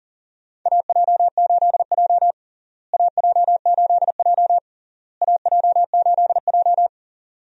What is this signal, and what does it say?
Morse code, heard as a keyed tone, about 40 wpm, 700 Hz. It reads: AJ8J AJ8J AJ8J
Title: USA Call Signs - 4 Characters - Repeated 3x - 40wpm